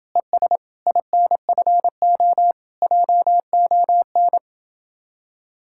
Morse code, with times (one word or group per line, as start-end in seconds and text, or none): 0.2-0.6 ES
0.9-2.5 INFO
2.8-4.4 JON